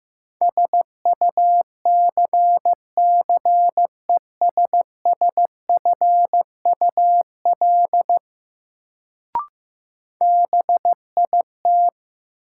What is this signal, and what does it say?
Morse code, heard as a keyed tone, about 15 wpm, 700 Hz. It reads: SUCCESSFUL BIT
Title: Sentences from Top 900 Words - New Word Drill - 15wpm